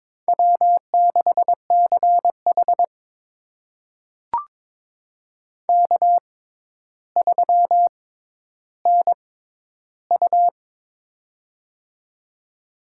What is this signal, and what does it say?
Morse code, heard as a keyed tone, about 22 wpm, spaced as for 11 wpm, 700 Hz. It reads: W6CH K3NU
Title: USA Call Signs - 4 Characters - 6x Character Spacing - 22wpm